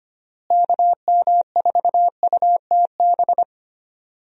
Morse code, heard as a keyed tone, about 25 wpm, 700 Hz. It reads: KM4UTB
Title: USA Call Signs - 6 Characters - Repeated 2x - 25wpm